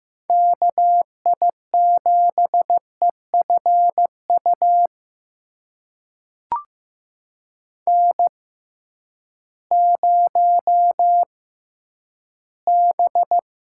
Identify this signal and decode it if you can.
Morse code, heard as a keyed tone, about 15 wpm, spaced as for 10 wpm, 700 Hz. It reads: KI7EFU N0B
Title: USA Call Signs - 6x Character Spacing - 15wpm